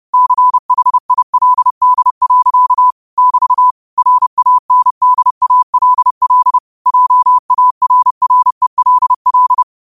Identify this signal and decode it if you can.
Morse code, heard as a keyed tone, about 30 wpm, 1000 Hz. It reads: CHILD. <BT> RANDALL JARRELL